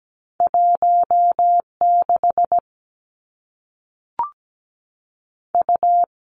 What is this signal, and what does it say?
Morse code, heard as a keyed tone, about 17 wpm, 700 Hz. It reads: 1 6 U
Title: Two Challenging Letter-Number Combinations 17wpm